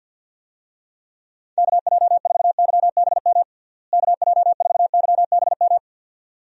Morse code, heard as a keyed tone, about 50 wpm, 700 Hz. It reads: KJ4YBM KJ4YBM
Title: USA Call Signs - 6 Characters - Repeated 2x - 50wpm